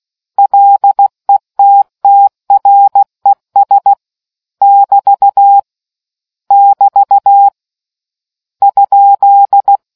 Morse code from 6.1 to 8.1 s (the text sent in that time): <BT>